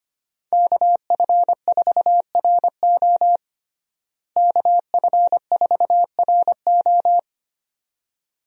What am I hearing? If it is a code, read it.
Morse code, heard as a keyed tone, about 25 wpm, 700 Hz. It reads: KF4RO KF4RO
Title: USA Call Signs - 5 Characters - Repeated 2x - 25wpm